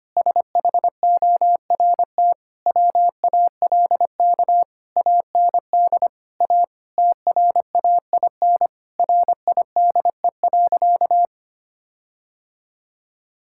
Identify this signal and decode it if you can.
Morse code, heard as a keyed tone, about 25 wpm, 700 Hz. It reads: SHORT WALK AND A TRAIN RIDE.
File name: practice / 25wpm / Difficult Sentences with Hints - 25wpm